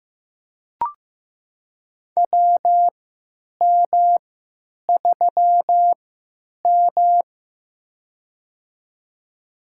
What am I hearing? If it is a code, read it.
Morse code, heard as a keyed tone, about 15 wpm, spaced as for 9 wpm, 700 Hz. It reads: WM3M
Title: USA Call Signs - 4 Characters - 3x Character Spacing - 15wpm